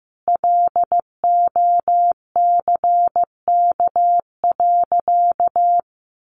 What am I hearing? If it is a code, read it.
Morse code, heard as a keyed tone, about 15 wpm, 700 Hz. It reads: LOCK.
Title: Sentences from Top 1300 Words 15wpm